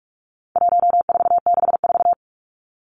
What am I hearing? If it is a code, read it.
Morse code, heard as a keyed tone, about 44 wpm, 700 Hz. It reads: 1464
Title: Numbers - 4-Digit 45wpm